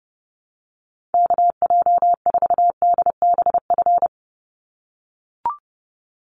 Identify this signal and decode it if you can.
Morse code, heard as a keyed tone, about 30 wpm, 700 Hz. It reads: KJ4DBF
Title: USA Call Signs 30wpm